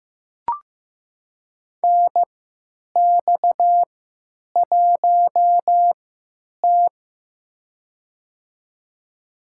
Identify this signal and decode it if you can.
Morse code, heard as a keyed tone, about 15 wpm, spaced as for 9 wpm, 700 Hz. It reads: NX1T
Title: USA Call Signs - 4 Characters - 3x Character Spacing - 15wpm